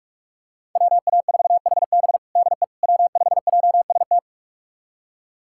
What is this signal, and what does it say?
Morse code, heard as a keyed tone, about 46 wpm, 700 Hz. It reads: WA4HB DE W5JST
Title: USA Callsign DE Callsign - 5 Characters - 45wpm